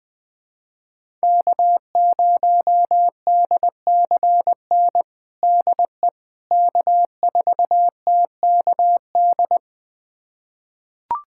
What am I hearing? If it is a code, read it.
Morse code, heard as a keyed tone, about 20 wpm, 700 Hz. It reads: K0DCN DE K4TKD
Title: USA Callsign DE Callsign - 5 Characters - 20wpm